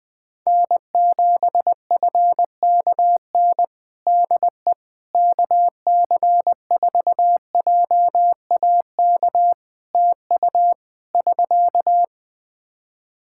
Morse code, heard as a keyed tone, about 20 wpm, 700 Hz. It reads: N7FKN DE KC4JAK TU <SK>
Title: Practice QSO - Exchange 4 - Ending - 20wpm